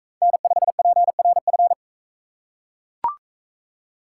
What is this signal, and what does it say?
Morse code, heard as a keyed tone, about 40 wpm, 700 Hz. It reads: N5PRF